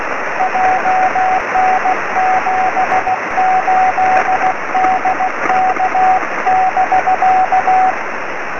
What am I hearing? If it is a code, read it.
Morse code, heard as a keyed tone, about 16 wpm, 740 Hz, with radio noise. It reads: JN78DK <BK>
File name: sm7lcb / 20 23cm / 2308 oe5vrl-5 23cm